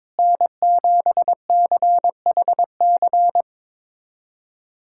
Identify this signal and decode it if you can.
Morse code, heard as a keyed tone, about 22 wpm, 700 Hz. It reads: N7CHC
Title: USA Callsign DE Callsign - 5 Characters - 22wpm